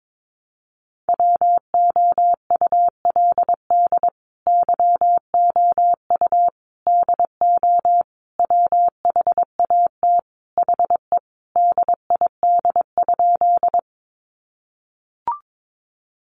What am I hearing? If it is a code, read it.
Morse code, heard as a keyed tone, about 22 wpm, 700 Hz. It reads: WOULD YOU DO WHAT HE DID?
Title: Sentences from Top 100 Words 22wpm